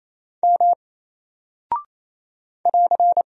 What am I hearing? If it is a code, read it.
Morse code, heard as a keyed tone, about 27 wpm, 700 Hz. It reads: M <AR>
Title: Single Character (Letter:Number:Punctuation:Prosign) 28wpm